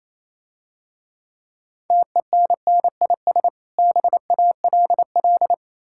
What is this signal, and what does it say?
Morse code, heard as a keyed tone, about 27 wpm, 690 Hz. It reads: TENNIS BALL
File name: Easy Words - Speed-Racing with Hints - 22wpm